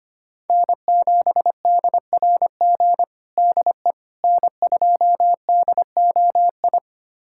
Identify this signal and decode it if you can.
Morse code, heard as a keyed tone, about 25 wpm, 700 Hz. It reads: N7DRG DE N2DOI
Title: USA Callsign DE Callsign - 5 Characters - 25wpm